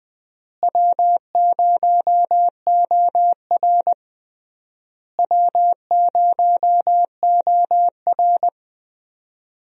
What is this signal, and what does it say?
Morse code, heard as a keyed tone, about 21 wpm, 700 Hz. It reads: W0OR W0OR
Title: USA Call Signs - 4 Characters - Repeated 3x - 20wpm